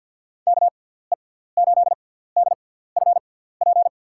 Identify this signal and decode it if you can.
Morse code, heard as a keyed tone, about 53 wpm, spaced as for 17 wpm, 700 Hz. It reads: KE8DFP
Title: USA Call Signs - 6x Character Spacing - 50wpm